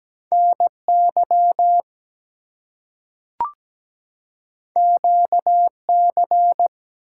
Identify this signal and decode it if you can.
Morse code, heard as a keyed tone, about 17 wpm, 700 Hz. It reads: NY QC
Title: US States and Canadian Provinces 17wpm